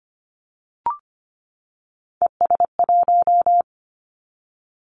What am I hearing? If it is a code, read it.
Morse code, heard as a keyed tone, about 25 wpm, 700 Hz. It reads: ES1
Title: Callsign Prefixes for Top 100 DXCC - 25wpm